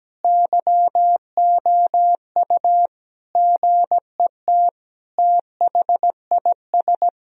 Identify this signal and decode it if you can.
Morse code, heard as a keyed tone, about 17 wpm, 700 Hz. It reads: YOU GET THIS
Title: Sentences from Top 1000 Words - Review Part 3 - 17wpm